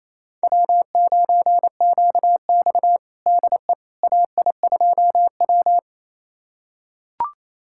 Morse code, heard as a keyed tone, about 28 wpm, 700 Hz. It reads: W9QX DE AI2W
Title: USA Callsign DE Callsign - 4 Characters - 28wpm